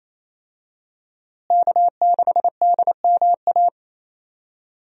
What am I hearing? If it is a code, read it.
Morse code, heard as a keyed tone, about 28 wpm, 700 Hz. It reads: K6DMA
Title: USA Call Signs - 5 Characters - Repeated 2x - 28wpm